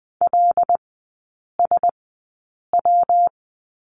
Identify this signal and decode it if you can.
Morse code, heard as a keyed tone, about 20 wpm, spaced as for 8 wpm, 690 Hz. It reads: LSW
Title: 172 Lesson 28e - Number 9 - Three Characters 20wpm